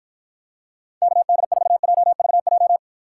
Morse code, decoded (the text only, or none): KD4JVJ